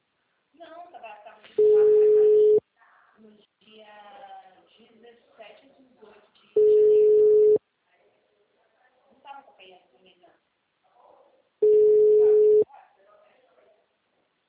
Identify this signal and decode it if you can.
Ringback tone (Europe, 425 Hz), with speech.
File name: out- -2007- - - .45186